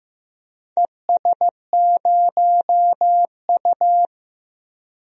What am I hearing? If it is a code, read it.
Morse code, heard as a keyed tone, about 15 wpm, 700 Hz. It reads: ES0U
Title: DXCC-100 Call Signs - 4 Characters - 15wpm